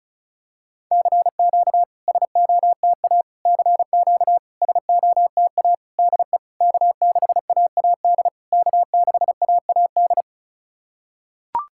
Morse code, heard as a keyed tone, about 34 wpm, 660 Hz. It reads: CQ SOTA CQ SOTA DE K6AAD K6AAD
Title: SOTA QSO Practice 35wpm